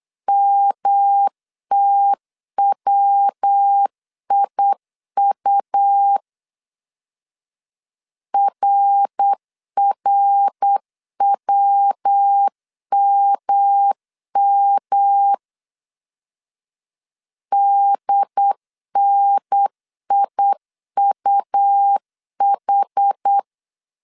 Morse code, heard as a keyed tone, about 8 wpm, 800 Hz. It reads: MTWIU RRWMM DNIUH
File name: CW / DET / 1 3 1500C S B 8wpm 4141